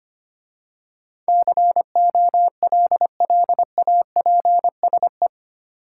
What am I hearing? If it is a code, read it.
Morse code, heard as a keyed tone, about 25 wpm, 700 Hz. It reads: COLLAPSE
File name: 8 Letter Words - First 3 Letters Given - 25wpm